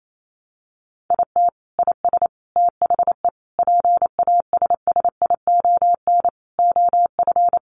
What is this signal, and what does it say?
Morse code, heard as a keyed tone, about 28 wpm, 700 Hz. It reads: IT IS THE PASSION OF
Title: Sentences from Top 1300 Words 28wpm